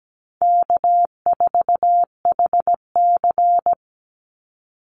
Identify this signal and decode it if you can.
Morse code, heard as a keyed tone, about 17 wpm, 700 Hz. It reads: K4HC
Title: Easy Call Signs Repeated 2x - 17wpm